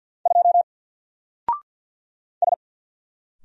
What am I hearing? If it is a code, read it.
Morse code, heard as a keyed tone, about 48 wpm, 700 Hz. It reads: 2 S
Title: Single Letter-Number 50wpm